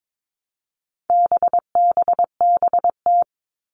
Morse code, heard as a keyed tone, about 22 wpm, 700 Hz. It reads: BBBT
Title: All Cut Numbers - 4-Digit 22wpm